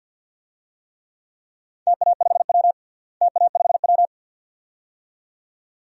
Morse code, heard as a keyed tone, about 53 wpm, 700 Hz. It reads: TA5W TA5W
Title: DXCC-100 Call Signs - 4 Characters - Repeated 2x - 50wpm